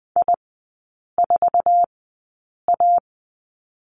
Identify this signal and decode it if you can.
Morse code, heard as a keyed tone, about 19 wpm, 700 Hz. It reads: I 4 A
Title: 047 Lesson 9e - Number 4 - Three Characters 20wpm